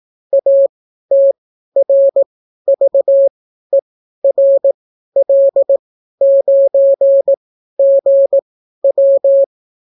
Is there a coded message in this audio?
Yes: Morse code from 0.3 to 9.4 s, heard as a keyed tone, about 18 wpm, spaced as for 12 wpm, 550 Hz. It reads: ATRVERL9GW